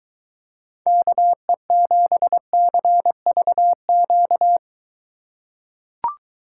Morse code, heard as a keyed tone, about 23 wpm, 700 Hz. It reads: KE7CVQ